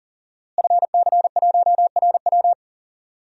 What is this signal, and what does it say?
Morse code, heard as a keyed tone, about 40 wpm, 700 Hz. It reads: FC1RW
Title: DXCC-100 Call Signs - 5 Characters - Repeated 3x - 40wpm